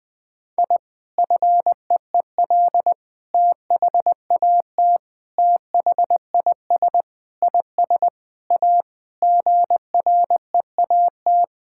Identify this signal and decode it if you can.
Morse code, heard as a keyed tone, about 21 wpm, 700 Hz. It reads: I FEEL THAT THIS IS A GREAT
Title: Sentences with 4-Word Phrases - 20wpm